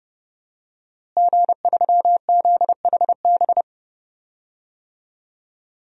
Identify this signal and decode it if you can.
Morse code, heard as a keyed tone, about 30 wpm, 700 Hz. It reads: G3ZHB
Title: DXCC-100 Call Signs - 5 Characters - Repeated 3x - 30wpm